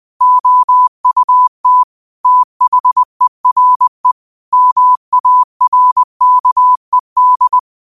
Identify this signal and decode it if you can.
Morse code, heard as a keyed tone, about 20 wpm, 1000 Hz. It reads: OUT THERE MARKED